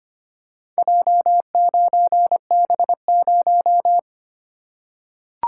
Morse code, heard as a keyed tone, about 25 wpm, 700 Hz. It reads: J9B0
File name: practice / 25wpm / Four Letter-Number Combinations - 25wpm